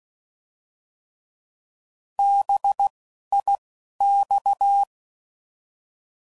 Morse code, heard as a keyed tone, about 16 wpm, 800 Hz. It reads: B I X